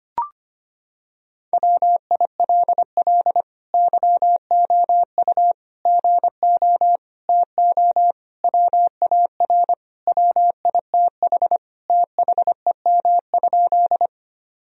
Morse code, heard as a keyed tone, about 25 wpm, 700 Hz. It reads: WILL YOU GO TO WAR WITH THEM?
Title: Sentences from Top 1000 Words - Review Part 2 - 25wpm